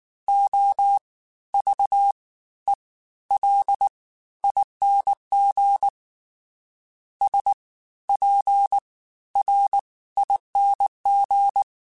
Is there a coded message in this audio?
Yes: Morse code from 0.3 to 11.6 s, heard as a keyed tone, about 19 wpm, spaced as for 13 wpm, 800 Hz. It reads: OVELING SPRING